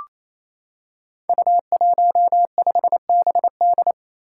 Morse code, heard as a keyed tone, about 27 wpm, 700 Hz. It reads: U15BD
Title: Five Challenging Letter-Number Combinations 28wpm